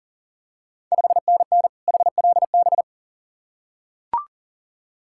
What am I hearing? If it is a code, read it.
Morse code, heard as a keyed tone, about 40 wpm, 690 Hz. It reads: 5NN HLB
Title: Ontario QSO Party Exchange 40wpm